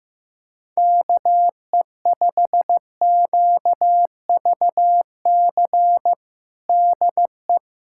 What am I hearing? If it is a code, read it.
Morse code, heard as a keyed tone, about 15 wpm, 700 Hz. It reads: KE5QVC DE
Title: USA Callsign DE Callsign 15wpm